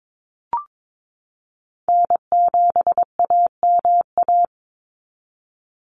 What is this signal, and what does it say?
Morse code, heard as a keyed tone, about 22 wpm, 700 Hz. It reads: N7AMA
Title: NA SOTA Chaser Callsigns 22wpm